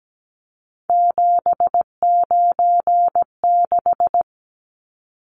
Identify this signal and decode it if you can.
Morse code, heard as a keyed tone, about 17 wpm, 700 Hz. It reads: 796
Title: Numbers - 3-Digit 17wpm